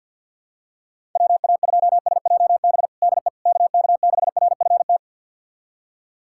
Morse code, heard as a keyed tone, about 50 wpm, 700 Hz. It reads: WA2SJB DE KK6RFT
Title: USA Callsign DE Callsign - 6 Characters - 50wpm